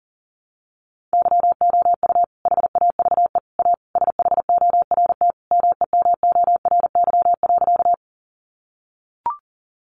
Morse code, heard as a keyed tone, about 40 wpm, 700 Hz. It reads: YOU HAVE A SHORT MEMORY.
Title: Sentences from Top 1300 Words 40wpm